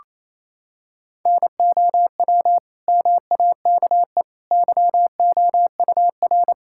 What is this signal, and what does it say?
Morse code, heard as a keyed tone, about 28 wpm, 700 Hz. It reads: NOW MAKE YOUR
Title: Sentences from Top 1000 Words - Review Part 6 - 28wpm